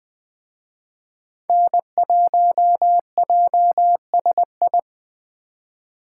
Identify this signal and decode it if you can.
Morse code, heard as a keyed tone, about 20 wpm, 700 Hz. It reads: N1JSI
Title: Speed Racing - USA Call Signs 17wpm